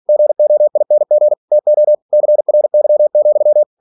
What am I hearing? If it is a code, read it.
Morse code, heard as a keyed tone, about 46 wpm, 600 Hz. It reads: GOING TO CRY,